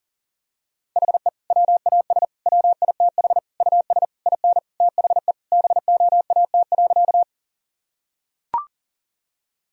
Morse code, heard as a keyed tone, about 40 wpm, 700 Hz. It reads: HE WAS WITH US IN THE BOAT.
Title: Sentences from Top 1000 Words - Review Part 3 - 40wpm